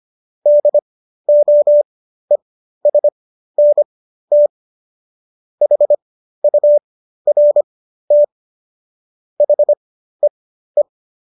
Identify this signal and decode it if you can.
Morse code, heard as a keyed tone, about 25 wpm, spaced as for 13 wpm, 590 Hz. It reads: DOESNT HURT HEE